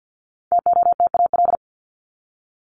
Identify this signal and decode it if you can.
Morse code, heard as a keyed tone, about 50 wpm, 700 Hz. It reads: TOTAL
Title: LY Suffix - Top 100 Words with Base Words 50wpm